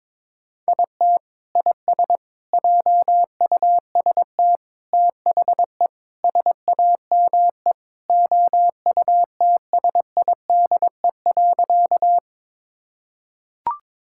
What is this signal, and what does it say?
Morse code, heard as a keyed tone, about 22 wpm, 700 Hz. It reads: IT IS JUST THE SAME OUTSIDE.
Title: Sentences from Top 1000 Words - Review Part 6 - 22wpm